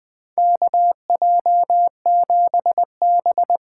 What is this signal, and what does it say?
Morse code, heard as a keyed tone, about 20 wpm, 700 Hz. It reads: KJ7B
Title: USA Call Signs - 6 Characters - 8x Character Spacing - 20wpm